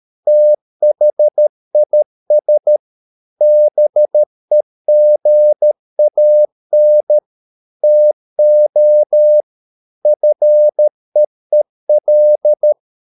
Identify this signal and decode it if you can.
Morse code, heard as a keyed tone, about 13 wpm, 600 Hz. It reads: THIS BEGAN TO FEEL